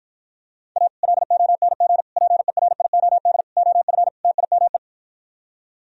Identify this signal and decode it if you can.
Morse code, heard as a keyed tone, about 52 wpm, 700 Hz. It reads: A LONG PERIOD OF TIME